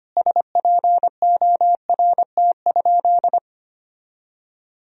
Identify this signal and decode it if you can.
Morse code, heard as a keyed tone, about 25 wpm, 700 Hz. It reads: SPORT?
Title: Sentences from Top 1000 Words - Review Part 4 - 25wpm